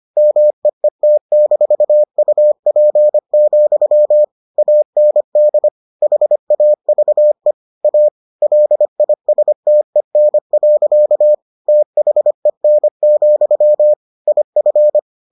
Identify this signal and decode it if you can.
Morse code, heard as a keyed tone, about 25 wpm, 600 Hz. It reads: MEET-UP, AND HAVE A LISTEN. THEN, IF